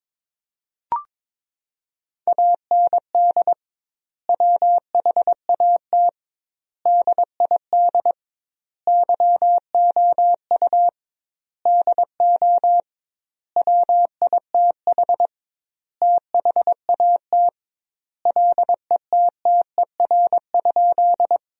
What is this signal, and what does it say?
Morse code, heard as a keyed tone, about 22 wpm, 700 Hz. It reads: AND WHAT DID YOU DO WITH THAT LETTER?
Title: 2x Word-Spacing - Sentences from Top 300 Words - 22wpm